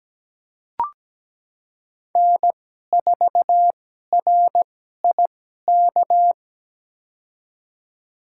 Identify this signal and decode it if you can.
Morse code, heard as a keyed tone, about 17 wpm, spaced as for 12 wpm, 700 Hz. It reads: N4RIK
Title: USA Call Signs - 5 Characters - 2x Character Spacing - 17wpm